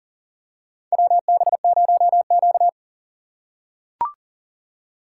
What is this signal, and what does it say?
Morse code, heard as a keyed tone, about 40 wpm, 700 Hz. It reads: WB0Q
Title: USA Call Signs - 4x Character Spacing - 40wpm